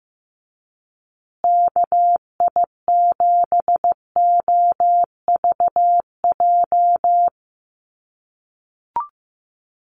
Morse code, heard as a keyed tone, about 15 wpm, 700 Hz. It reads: KI7OVJ
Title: USA Call Signs Repeated 3x 15wpm